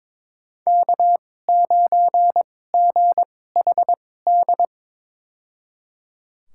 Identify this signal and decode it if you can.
Morse code, heard as a keyed tone, about 21 wpm, spaced as for 16 wpm, 700 Hz. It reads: K9GHD
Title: USA Call Signs - 2x Character Spacing - 22wpm